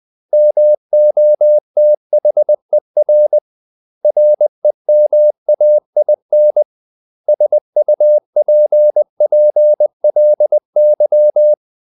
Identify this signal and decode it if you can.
Morse code, heard as a keyed tone, about 20 wpm, 600 Hz. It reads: MOTHER REMAIN SUPPLY
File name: CW / Practice / 500MostCommonEnglishWords20